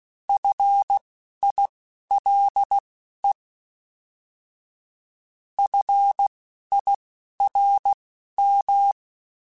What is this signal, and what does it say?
Morse code, heard as a keyed tone, about 16 wpm, spaced as for 11 wpm, 800 Hz. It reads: FILE FIRM
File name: Alphabetical / Alphabetical mp3s / Fs @10